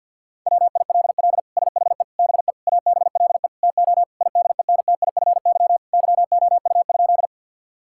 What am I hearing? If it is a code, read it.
Morse code, heard as a keyed tone, about 50 wpm, 700 Hz. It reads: WILL SHE BE ABLE TO IDENTIFY YOU?
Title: Sentences from Top 1000 Words - Review Part 6 - 50wpm